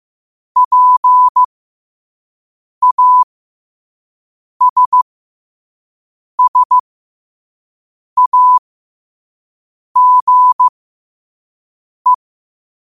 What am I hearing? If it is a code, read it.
Morse code, heard as a keyed tone, about 15 wpm, spaced as for 5 wpm, 1000 Hz. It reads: PASSAGE
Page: Quotes for Wed, 13 Aug 2025 in Morse Code at 5 words per minute.